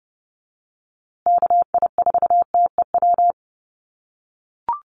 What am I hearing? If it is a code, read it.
Morse code, heard as a keyed tone, about 30 wpm, 700 Hz. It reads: KI4TEW
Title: USA Call Signs Repeated 3x 30wpm